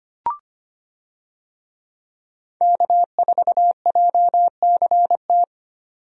Morse code, heard as a keyed tone, about 25 wpm, 700 Hz. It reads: K4JCT
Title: USA Call Signs - 5 Characters - Repeated 3x - 25wpm